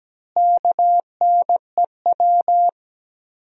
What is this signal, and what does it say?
Morse code, heard as a keyed tone, about 17 wpm, 700 Hz. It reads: KNEW
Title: 4x Word-Spacing - Sentences from Top 400 Words - 17wpm